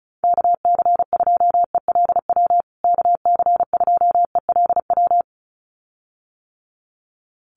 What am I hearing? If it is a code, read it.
Morse code, heard as a keyed tone, about 34 wpm, 700 Hz. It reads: KC2ELW KC2ELW
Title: USA Call Signs Repeated 2x 35wpm